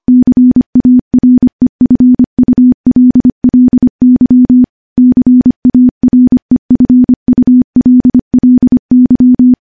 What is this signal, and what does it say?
Morse code, heard as a keyed tone, about 25 wpm, 250 Hz. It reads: CAREFULLY CAREFULLY
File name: morse / morse-training-25WPM